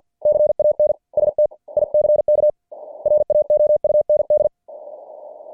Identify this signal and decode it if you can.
Morse code, heard as a keyed tone, about 50 wpm, spaced as for 43 wpm, 590 Hz. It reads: CAN STICK RAOUND
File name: archived / CWBros